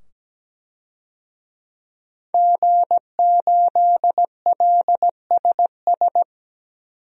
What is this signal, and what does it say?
Morse code, heard as a keyed tone, about 17 wpm, 700 Hz. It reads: G8LSS
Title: DXCC-100 Call Signs - 5 Characters - 17wpm